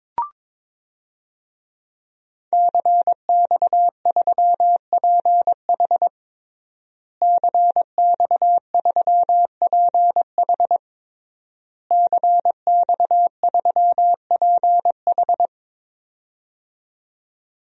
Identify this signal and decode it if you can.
Morse code, heard as a keyed tone, about 21 wpm, 700 Hz. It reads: CX3PH CX3PH CX3PH
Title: DXCC-100 Call Signs - 5 Characters - Repeated 3x - 22wpm